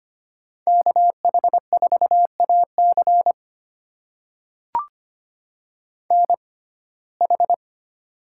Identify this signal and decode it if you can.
Morse code, heard as a keyed tone, about 25 wpm, spaced as for 18 wpm, 700 Hz. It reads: KH4AC NH
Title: USA Call Signs - 5 Characters - 6x Character Spacing - 25wpm